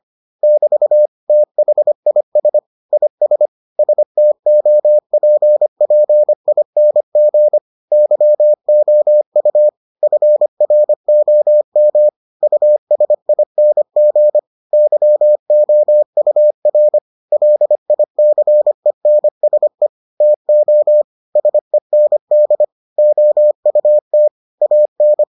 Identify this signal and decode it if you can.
Morse code, heard as a keyed tone, about 25 wpm, 600 Hz. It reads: <BT> THIS IS STOPPING YOU FROM USING YOUR LICENSE TO SEND OUT AN